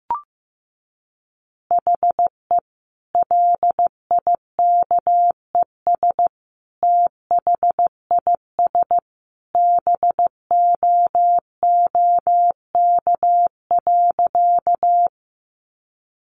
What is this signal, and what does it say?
Morse code, heard as a keyed tone, about 14 wpm, 700 Hz. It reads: HE LIKES THIS BOOK.
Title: Additional Easy Phrases 15wpm